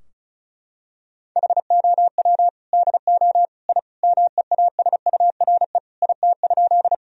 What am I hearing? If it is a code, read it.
Morse code, heard as a keyed tone, about 36 wpm, 700 Hz. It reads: HOW DO I MEASURE IT?
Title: Sentences from Top 1000 Words - Review Part 2 - 35wpm